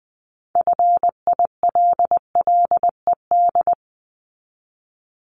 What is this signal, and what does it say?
Morse code, heard as a keyed tone, about 20 wpm, 700 Hz. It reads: FILLED
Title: Multi-letter Phonemes 20wpm